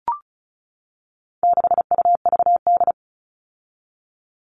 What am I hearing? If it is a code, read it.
Morse code, heard as a keyed tone, about 34 wpm, 700 Hz. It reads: 6UVD
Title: All Cut Numbers - 4-Digit 35wpm